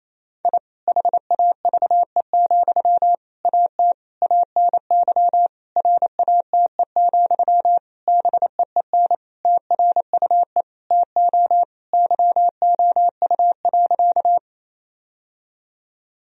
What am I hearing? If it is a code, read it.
Morse code, heard as a keyed tone, about 28 wpm, 700 Hz. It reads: I HAVE, AT ANY RATE, BEEN TRUE TO YOU.
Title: Sentences from Top 1000 Words - Review Part 4 - 28wpm